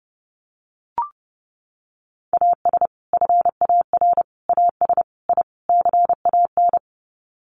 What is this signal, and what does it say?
Morse code, heard as a keyed tone, about 30 wpm, 700 Hz. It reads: AS FAR AS I CAN
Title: Sets of 5 Words Encore 30wpm